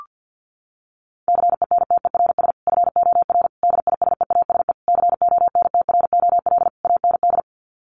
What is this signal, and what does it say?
Morse code, heard as a keyed tone, about 50 wpm, 700 Hz. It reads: CENTERS FOR DISEASE CONTROL AND
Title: Sets of 5 Words Encore2 50wpm